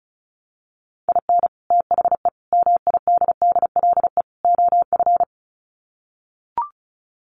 Morse code, heard as a keyed tone, about 34 wpm, 700 Hz. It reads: IN THE MIDDLE OF
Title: Sets of 4 Words 35wpm